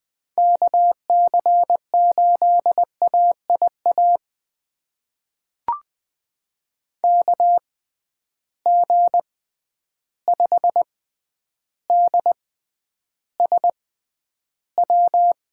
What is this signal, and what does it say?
Morse code, heard as a keyed tone, about 20 wpm, spaced as for 10 wpm, 700 Hz. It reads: KC8AIA KG5DSW
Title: USA Call Signs - 6 Characters - 6x Character Spacing - 20wpm